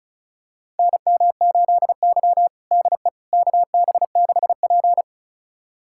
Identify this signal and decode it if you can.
Morse code, heard as a keyed tone, about 36 wpm, 690 Hz. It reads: NM8Y DE KB6P
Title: USA Callsign DE Callsign - 4 Characters - 35wpm